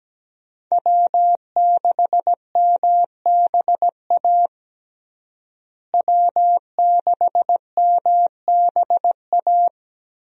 Morse code, heard as a keyed tone, about 17 wpm, 700 Hz. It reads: W6MBA W6MBA
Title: USA Call Signs - 5 Characters - Repeated 2x - 17wpm